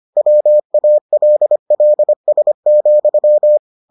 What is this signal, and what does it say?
Morse code, heard as a keyed tone, about 25 wpm, 600 Hz. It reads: WALLS,